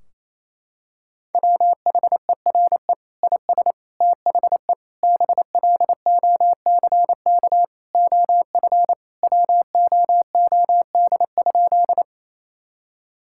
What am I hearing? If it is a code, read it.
Morse code, heard as a keyed tone, about 28 wpm, 700 Hz. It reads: WHERE IS THE BLOCK OF WOOD?
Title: Sentences from Top 1600 Words 28wpm